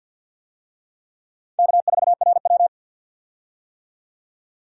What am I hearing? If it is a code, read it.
Morse code, heard as a keyed tone, about 51 wpm, 700 Hz. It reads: K4RW
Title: USA Call Signs - 4 Characters - 50wpm